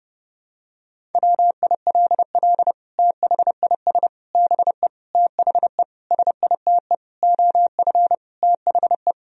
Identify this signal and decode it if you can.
Morse code, heard as a keyed tone, about 30 wpm, 700 Hz. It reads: WILL THIS BE THE SITE OF THE
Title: Sentences from Top 1000 Words - Review Part 1 - 30wpm